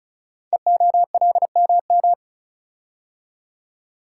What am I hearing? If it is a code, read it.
Morse code, heard as a keyed tone, about 36 wpm, 700 Hz. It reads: EOLMM
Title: Code Groups - 5 Letters - 35wpm